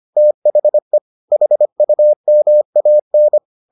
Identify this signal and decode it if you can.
Morse code, heard as a keyed tone, about 25 wpm, 600 Hz. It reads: THE HUMAN